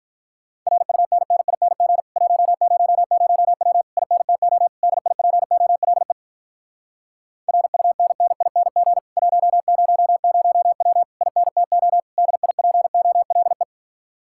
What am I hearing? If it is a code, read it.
Morse code, heard as a keyed tone, about 48 wpm, 700 Hz. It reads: RUNNING 100W INTO DIPOLE RUNNING 100W INTO DIPOLE